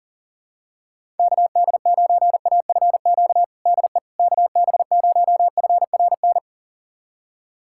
Morse code, heard as a keyed tone, about 40 wpm, 700 Hz. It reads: KD9AFQ DE KB0FRN
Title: USA Callsign DE Callsign - 6 Characters - 40wpm